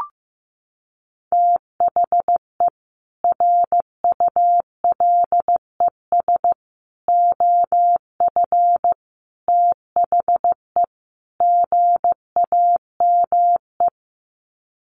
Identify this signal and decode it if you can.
Morse code, heard as a keyed tone, about 15 wpm, 700 Hz. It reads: THE RULES OF THE GAME
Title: Sets of 5 Words Encore2 15wpm